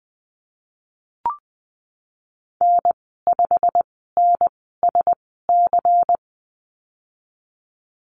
Morse code, heard as a keyed tone, about 20 wpm, spaced as for 14 wpm, 700 Hz. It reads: N5NSC
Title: 147 Lesson 24g - Letter y - Call Signs 20wpm